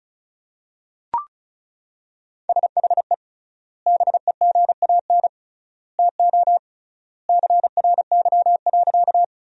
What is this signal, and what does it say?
Morse code, heard as a keyed tone, about 34 wpm, 700 Hz. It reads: SHE BEGAN TO CRY.
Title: 3x Word-Spacing - Sentences from Top 500 Words - 35wpm